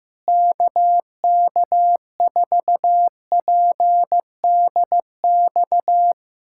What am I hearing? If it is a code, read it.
Morse code, heard as a keyed tone, about 16 wpm, 700 Hz. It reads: KK4PDX
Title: USA Callsign DE Callsign - 6 Characters - 15wpm